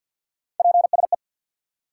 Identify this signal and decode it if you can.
Morse code, heard as a keyed tone, about 53 wpm, 700 Hz. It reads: PSE
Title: Top 90 QSO Elements 50wpm